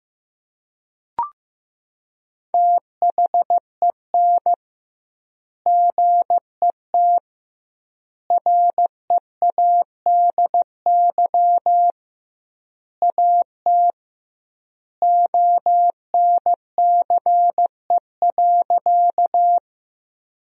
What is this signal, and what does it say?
Morse code, heard as a keyed tone, about 16 wpm, 700 Hz. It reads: THEN GET READY AT ONCE.
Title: 2x Word-Spacing - Sentences from Top 400 Words - 15wpm